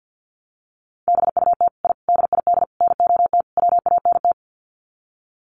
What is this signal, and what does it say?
Morse code, heard as a keyed tone, about 50 wpm, 700 Hz. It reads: BUT I DID NOT WANT
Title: Sets of 5 Words Encore 50wpm